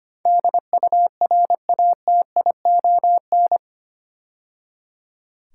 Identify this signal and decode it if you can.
Morse code, heard as a keyed tone, about 25 wpm, 700 Hz. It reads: DURATION
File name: 8 Letter Words - First 3 Letters Given - 25wpm